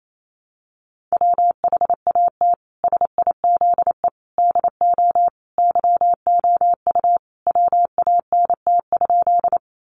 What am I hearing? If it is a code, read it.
Morse code, heard as a keyed tone, about 28 wpm, 700 Hz. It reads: WHAT SIZE DO YOU WANT?
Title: Sentences from Top 400 Words 28wpm